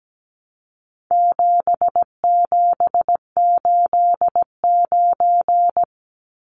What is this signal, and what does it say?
Morse code, heard as a keyed tone, about 17 wpm, 700 Hz. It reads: 7789
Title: Numbers - 4-Digit 17wpm